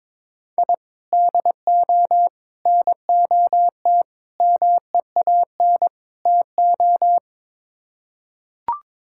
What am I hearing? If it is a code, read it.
Morse code, heard as a keyed tone, about 22 wpm, 700 Hz. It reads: I DO NOT MEAN TO
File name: practice / 22wpm / Build A Phrase - 5-Word Phrases - 22wpm